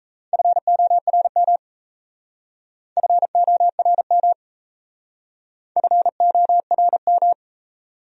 Morse code, heard as a keyed tone, about 35 wpm, 700 Hz. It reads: FORM FORM FORM
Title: Speed-Racing Top 500 Words 28wpm